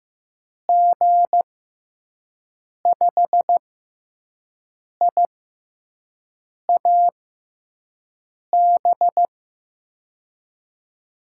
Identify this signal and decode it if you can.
Morse code, heard as a keyed tone, about 15 wpm, spaced as for 5 wpm, 700 Hz. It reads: G5IAB
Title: USA Call Signs - 6 Characters - 6x Character Spacing - 15wpm